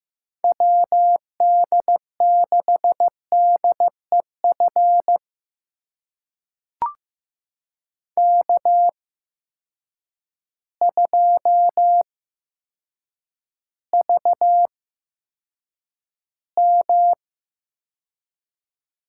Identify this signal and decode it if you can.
Morse code, heard as a keyed tone, about 15 wpm, spaced as for 8 wpm, 700 Hz. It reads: WD6DEF K2VM
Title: USA Call Signs - 8x Character Spacing - 15wpm